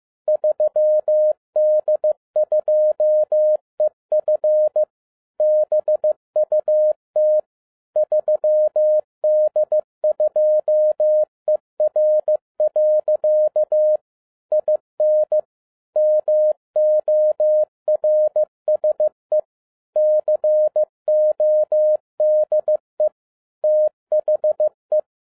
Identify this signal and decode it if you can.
Morse code, heard as a keyed tone, about 15 wpm, 610 Hz. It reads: 3D2EF BUT 3D2ER. IN MORSE CODE THE
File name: 2011-10-mp3 / n07-A DOT 15wpm.mp3